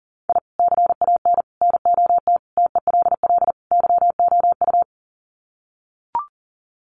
Morse code, heard as a keyed tone, about 40 wpm, 700 Hz. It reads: I CAN NOT TELL YOU
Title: Sets of 5 Words 40wpm